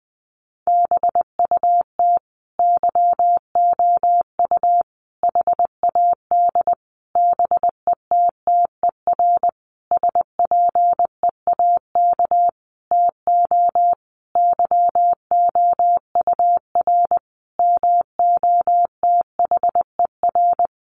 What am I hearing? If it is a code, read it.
Morse code, heard as a keyed tone, about 20 wpm, 700 Hz. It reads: BUT YOU HAD BETTER SPEAK TO YOUR MOTHER
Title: Sentences from Top 1100 Words 20wpm